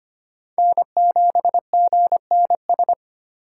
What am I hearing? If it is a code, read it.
Morse code, heard as a keyed tone, about 25 wpm, 700 Hz. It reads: N7GNS
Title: USA Call Signs - 5 Characters - 3x Character Spacing - 25wpm